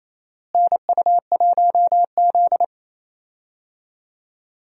Morse code, heard as a keyed tone, about 28 wpm, 700 Hz. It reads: NU1Z
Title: USA Call Signs - 4 Characters - Repeated 2x - 28wpm